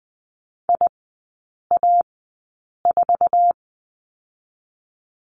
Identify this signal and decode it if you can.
Morse code, heard as a keyed tone, about 19 wpm, spaced as for 8 wpm, 690 Hz. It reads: IA4
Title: 087 Lesson 15e - Number 5 - Three Characters 20wpm